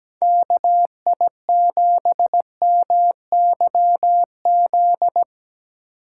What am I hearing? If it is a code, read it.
Morse code, heard as a keyed tone, about 17 wpm, 700 Hz. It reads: KI7MYZ
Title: USA Call Signs - 6 Characters - 3x Character Spacing - 17wpm